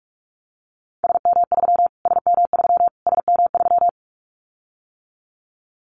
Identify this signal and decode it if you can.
Morse code, heard as a keyed tone, about 46 wpm, 700 Hz. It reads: SM3 SM3 SM3
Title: Callsign Prefixes for Top 25 DXCC - Repeated 3x - 45wpm